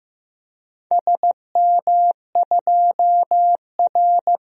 Morse code, heard as a keyed tone, about 15 wpm, 700 Hz. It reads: SM2R
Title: DXCC-100 Call Signs - 4 Characters - 15wpm